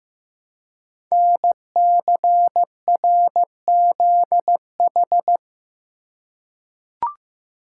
Morse code, heard as a keyed tone, about 15 wpm, 700 Hz. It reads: NCRZH
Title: Code Groups - 5 Letters - 15wpm